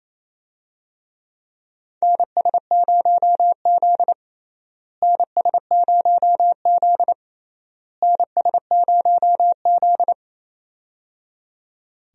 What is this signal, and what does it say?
Morse code, heard as a keyed tone, about 28 wpm, 700 Hz. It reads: NS0Z NS0Z NS0Z
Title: USA Call Signs - 4 Characters - Repeated 3x - 28wpm